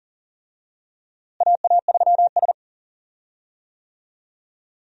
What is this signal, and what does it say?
Morse code, heard as a keyed tone, about 40 wpm, 700 Hz. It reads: AA3S
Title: USA Call Signs - 4 Characters - Repeated 2x - 40wpm